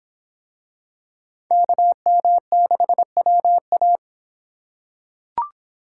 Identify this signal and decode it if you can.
Morse code, heard as a keyed tone, about 25 wpm, 700 Hz. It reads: KM6WA